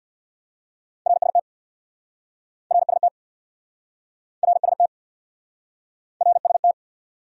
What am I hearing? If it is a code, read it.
Morse code, heard as a keyed tone, about 69 wpm, 700 Hz. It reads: RST RST RST RST